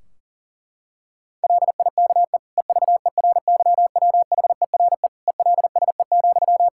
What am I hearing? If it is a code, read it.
Morse code, heard as a keyed tone, about 40 wpm, 700 Hz. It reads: LIKE EVERYWHERE ELSE,
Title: Sentences from Top 1600 Words 40wpm